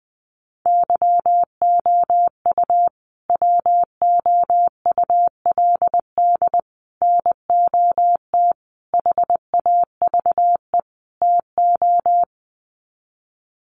Morse code, heard as a keyed tone, about 20 wpm, 700 Hz. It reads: YOU WOULD NOT HAVE TO
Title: Sets of 5 Words Encore 20wpm